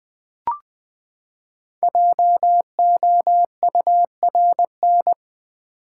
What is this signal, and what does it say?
Morse code, heard as a keyed tone, about 20 wpm, 700 Hz. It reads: JOURN
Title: Build A Word - 5 Letters At A Time - 10-Letter Words - 20wpm